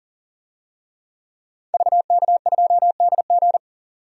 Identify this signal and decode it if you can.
Morse code, heard as a keyed tone, about 37 wpm, 700 Hz. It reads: VK2DG
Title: DXCC-100 Call Signs - 5 Characters - 40wpm